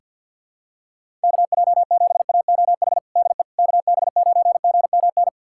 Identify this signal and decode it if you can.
Morse code, heard as a keyed tone, about 47 wpm, 700 Hz. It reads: KJ7AOH DE KB9GMN
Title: USA Callsign DE Callsign - 6 Characters - 50wpm